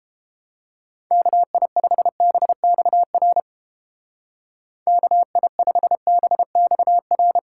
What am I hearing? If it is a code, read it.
Morse code, heard as a keyed tone, about 30 wpm, 700 Hz. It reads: KI5BXR KI5BXR